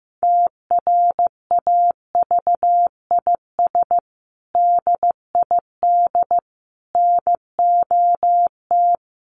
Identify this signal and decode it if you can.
Morse code, heard as a keyed tone, about 15 wpm, 700 Hz. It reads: TRAVIS DID NOT
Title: A Sound Of Thunder 15wpm